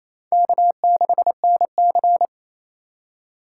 Morse code, heard as a keyed tone, about 36 wpm, 700 Hz. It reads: K6NC